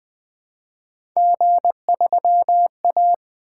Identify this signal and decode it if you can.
Morse code, heard as a keyed tone, about 20 wpm, 700 Hz. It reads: G3A
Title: Three Letter-Number Combinations - 20wpm